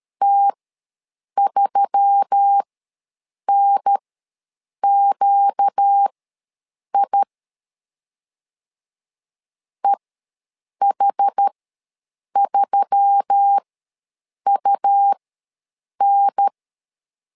Morse code, heard as a keyed tone, about 13 wpm, spaced as for 7 wpm, 800 Hz. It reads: T3NQI EH3UN
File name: LEZIONI / 4 Q1S3 ws 12wpm 2038